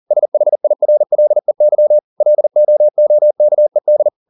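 Morse code, heard as a keyed tone, about 40 wpm, 600 Hz. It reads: SHIRLEY LOOKED
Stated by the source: Morse code at 40 wpm, practice text, CW med Gnister.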